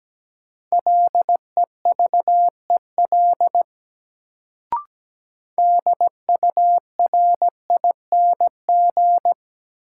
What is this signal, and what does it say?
Morse code, heard as a keyed tone, about 17 wpm, 700 Hz. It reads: LEVEL DURING
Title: Sentences from Top 700 Words - New Word Drill - 17wpm